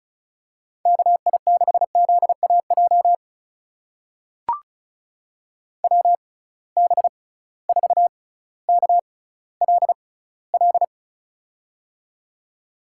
Morse code, heard as a keyed tone, about 36 wpm, spaced as for 18 wpm, 700 Hz. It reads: KI6ZAJ WB4KLL
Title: USA Call Signs - 6 Characters - 6x Character Spacing - 35wpm